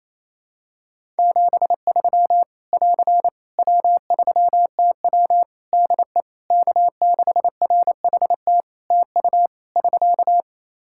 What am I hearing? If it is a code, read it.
Morse code, heard as a keyed tone, about 28 wpm, 700 Hz. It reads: 73 <AR> W3TW DE K6RHT TU <SK>
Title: Practice QSO - Exchange 4 - Ending - 28wpm